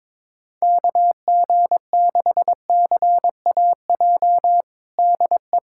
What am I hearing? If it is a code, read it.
Morse code, heard as a keyed tone, about 21 wpm, 700 Hz. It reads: KG6CAJ DE
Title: USA Callsign DE Callsign - 6 Characters - 22wpm